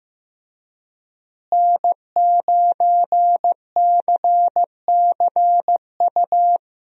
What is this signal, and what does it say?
Morse code, heard as a keyed tone, about 16 wpm, 700 Hz. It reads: N9CCU
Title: USA Call Signs - 5 Characters - Repeated 3x - 15wpm